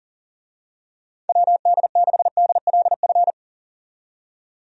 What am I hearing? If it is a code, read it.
Morse code, heard as a keyed tone, about 40 wpm, 700 Hz. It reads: WD6DLF
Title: USA Call Signs - 6 Characters - 40wpm